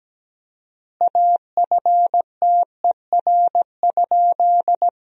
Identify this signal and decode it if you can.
Morse code, heard as a keyed tone, about 17 wpm, 700 Hz. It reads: AFTER?
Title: 4x Word-Spacing - Sentences from Top 200 Words - 17wpm